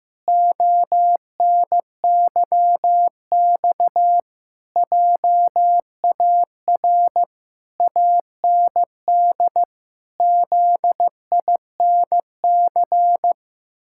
Morse code, heard as a keyed tone, about 15 wpm, 700 Hz. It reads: ONYX JAR AND ZINC
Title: Pangram Challenge v2 - 15wpm